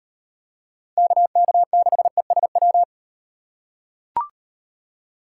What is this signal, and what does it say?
Morse code, heard as a keyed tone, about 38 wpm, 700 Hz. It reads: KK6ESW